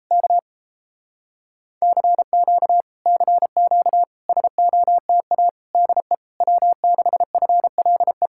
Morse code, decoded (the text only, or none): K CQ CQ SOTA DE W6FLE